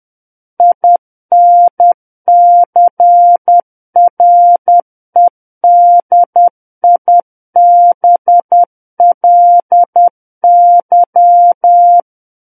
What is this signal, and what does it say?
Morse code, heard as a keyed tone, about 10 wpm, 700 Hz. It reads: INCREDIBLY
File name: Book One / 01-Chapter 1 - 10 WPM